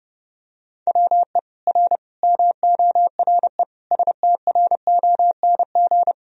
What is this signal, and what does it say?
Morse code, heard as a keyed tone, about 30 wpm, 700 Hz. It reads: WE R MORE STRONG
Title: Sentences from Top 500 Words and Abbreviations 30wpm